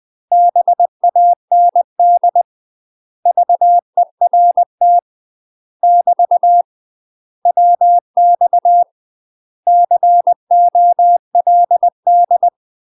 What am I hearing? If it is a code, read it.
Morse code, heard as a keyed tone, about 20 wpm, 700 Hz. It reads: BAND VERT <BT> WX COLD
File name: UK MASTER QSOs TRAINING / 20WPM / UK MASTER QSOs 0058